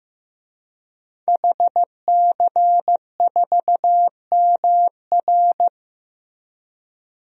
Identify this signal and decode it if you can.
Morse code, heard as a keyed tone, about 15 wpm, 700 Hz. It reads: HC4MR
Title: DXCC-100 Call Signs - 5 Characters - Repeated 2x - 15wpm